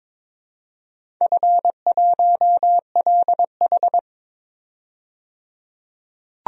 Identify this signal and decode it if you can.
Morse code, heard as a keyed tone, about 22 wpm, 700 Hz. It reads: F1LH
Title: DXCC-100 Call Signs - 4 Characters - 22wpm